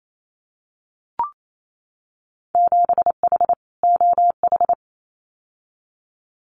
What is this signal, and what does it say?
Morse code, heard as a keyed tone, about 28 wpm, 700 Hz. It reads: 7H OH
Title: Winter Field Day Exchanges Encore 28wpm